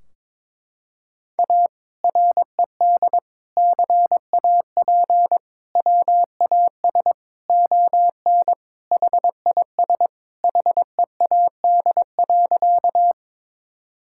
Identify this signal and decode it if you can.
Morse code, heard as a keyed tone, about 22 wpm, 700 Hz. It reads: A RED CAP WAS ON HIS HEAD.
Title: Sentences from Top 1600 Words 22wpm